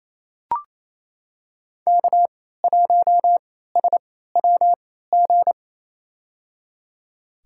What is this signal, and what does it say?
Morse code, heard as a keyed tone, about 28 wpm, spaced as for 16 wpm, 700 Hz. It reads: K1SWG
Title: USA Call Signs - 3x Character Spacing - 28wpm